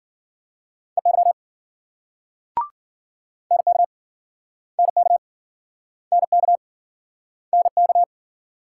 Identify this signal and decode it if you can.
Morse code, heard as a keyed tone, about 53 wpm, 700 Hz. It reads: EY NK NK NK NK